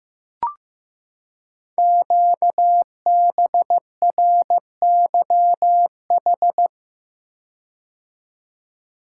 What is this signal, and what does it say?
Morse code, heard as a keyed tone, about 15 wpm, 700 Hz. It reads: QBRYH
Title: Code Groups - 5 Letters - 15wpm